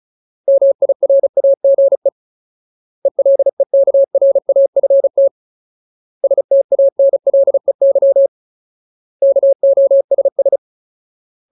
Morse code, heard as a keyed tone, about 35 wpm, 550 Hz. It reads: MIRAGE ELEKRAFT STANLEY KOSS
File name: Ultimate Word List / 35WPM Ultimate